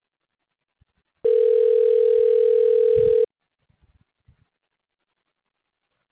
Ringback tone (North America), 440 + 480 Hz.